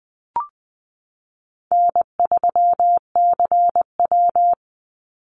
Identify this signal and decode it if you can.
Morse code, heard as a keyed tone, about 20 wpm, 700 Hz. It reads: N3CW
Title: Easy Call Signs Repeated 3x - 20wpm